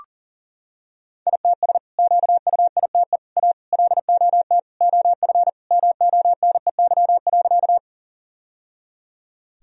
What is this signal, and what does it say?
Morse code, heard as a keyed tone, about 40 wpm, 700 Hz. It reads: ITS QUITE A LOT OF MONEY.
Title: Sentences from Top 1000 Words - Review Part 3 - 40wpm